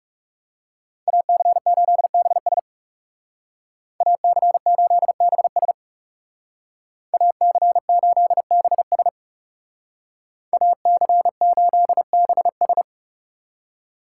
Morse code, heard as a keyed tone, about 40 wpm, 700 Hz. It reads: AC8BS AC8BS AC8BS AC8BS